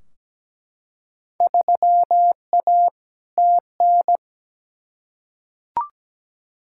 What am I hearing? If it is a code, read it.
Morse code, heard as a keyed tone, about 17 wpm, 700 Hz. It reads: 3A TN
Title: Field Day Exchanges - 17wpm